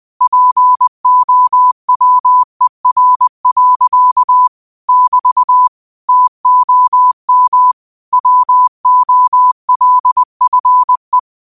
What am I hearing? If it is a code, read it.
Morse code, heard as a keyed tone, about 20 wpm, 1000 Hz. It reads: POWER. <BT> TOM WOLFE
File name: qotdfiles / 20wpm